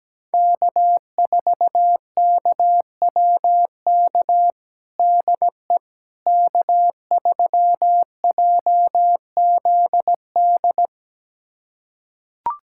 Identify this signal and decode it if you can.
Morse code, heard as a keyed tone, about 17 wpm, 700 Hz. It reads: K4KWK DE K3JZD
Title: USA Callsign DE Callsign - 5 Characters - 17wpm